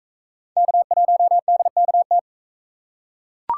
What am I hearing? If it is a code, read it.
Morse code, heard as a keyed tone, about 41 wpm, 700 Hz. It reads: K1DKT